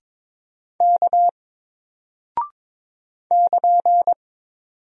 Morse code, heard as a keyed tone, about 22 wpm, 700 Hz. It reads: K <KN>
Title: Single Character (Letter:Number:Punctuation:Prosign) Alt 22wpm